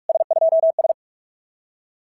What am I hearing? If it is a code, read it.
Morse code, heard as a keyed tone, about 44 wpm, 660 Hz. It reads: S2S
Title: SOTA QSO Practice 45wpm